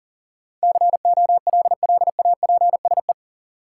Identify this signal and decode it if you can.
Morse code, heard as a keyed tone, about 40 wpm, 700 Hz. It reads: COLLAPSE
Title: Sets of 5 Words Encore - New Word Drill - 40wpm